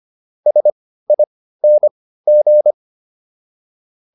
Morse code, heard as a keyed tone, about 25 wpm, spaced as for 15 wpm, 600 Hz. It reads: SING